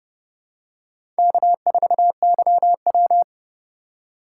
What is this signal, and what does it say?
Morse code, heard as a keyed tone, about 32 wpm, 700 Hz. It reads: K4YW
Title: USA Call Signs - 4 Characters - 2x Character Spacing - 30wpm